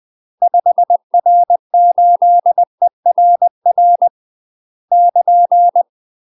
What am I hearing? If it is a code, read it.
Morse code, heard as a keyed tone, about 20 wpm, 700 Hz. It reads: 5R8ERR <KN>
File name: UK MASTER QSOs TRAINING / 20WPM / UK MASTER QSOs 0062